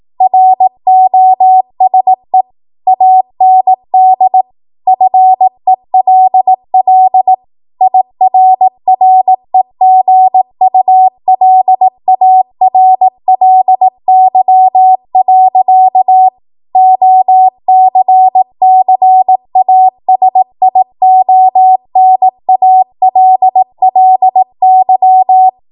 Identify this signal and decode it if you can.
Morse code, heard as a keyed tone, about 18 wpm, 750 Hz. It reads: ROSE AND FELL IRREGULARLY. OCCASIONALLY
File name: Morse / 18WPM